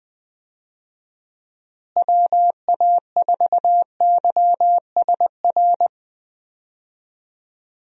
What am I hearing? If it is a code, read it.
Morse code, heard as a keyed tone, about 20 wpm, 700 Hz. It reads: WA4YSR
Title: USA Call Signs - 6 Characters - Repeated 3x - 20wpm